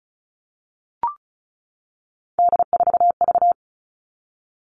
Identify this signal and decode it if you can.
Morse code, heard as a keyed tone, about 34 wpm, 700 Hz. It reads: D4V
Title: All Cut Numbers - 3-Digit 35wpm